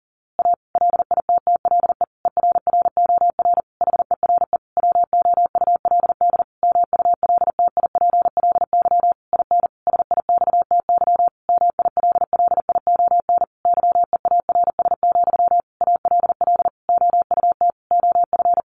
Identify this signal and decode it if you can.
Morse code, heard as a keyed tone, about 40 wpm, 700 Hz. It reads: A LITTLE ERROR HERE WOULD MULTIPLY IN SIXTY MILLION YEARS, ALL OUT OF
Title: A Sound Of Thunder 40wpm